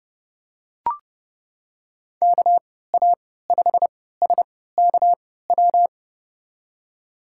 Morse code, heard as a keyed tone, about 31 wpm, spaced as for 17 wpm, 700 Hz. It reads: KA5SKW
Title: USA Call Signs - 6 Characters - 3x Character Spacing - 30wpm